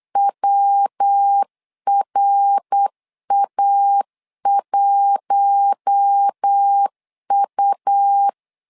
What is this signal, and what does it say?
Morse code, heard as a keyed tone, about 8 wpm, 800 Hz. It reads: WRA1U
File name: LEZIONI / 4 Q1S 8wpm 2710